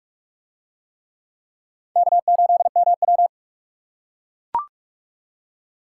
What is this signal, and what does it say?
Morse code, heard as a keyed tone, about 38 wpm, 700 Hz. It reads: K8MW
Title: Speed Racing - USA Call Signs 30wpm